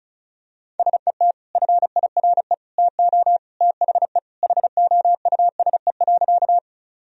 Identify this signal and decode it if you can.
Morse code, heard as a keyed tone, about 34 wpm, 700 Hz. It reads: SET FIRE TO THE HOUSE.
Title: Sentences from Top 1000 Words - Review Part 1 - 35wpm